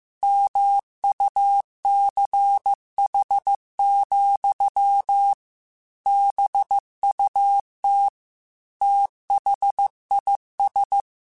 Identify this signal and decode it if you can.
Morse code, heard as a keyed tone, about 15 wpm, 800 Hz. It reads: MUCH, BUT THIS
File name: QSOs / QSOs-mp3s / WD0BGP 2@13